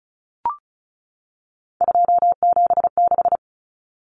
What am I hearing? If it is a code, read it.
Morse code, heard as a keyed tone, about 34 wpm, 700 Hz. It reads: 276
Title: Numbers - 3-Digit 35wpm